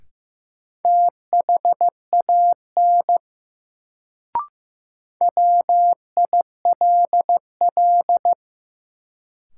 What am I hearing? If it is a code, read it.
Morse code, heard as a keyed tone, about 15 wpm, 700 Hz. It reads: THAN WILL
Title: Sentences from Top 100 Words - New Word Drill - 15wpm